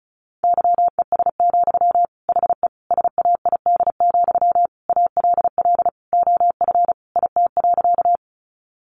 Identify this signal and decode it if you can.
Morse code, heard as a keyed tone, about 34 wpm, 700 Hz. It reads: YES, HE SAID, ALL OF IT.
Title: Sentences from Top 500 Words 35wpm